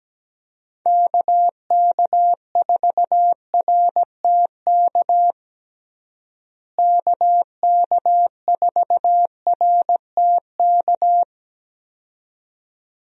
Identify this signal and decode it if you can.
Morse code, heard as a keyed tone, about 17 wpm, 700 Hz. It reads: KK4RTK KK4RTK
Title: USA Call Signs - 6 Characters - Repeated 2x - 17wpm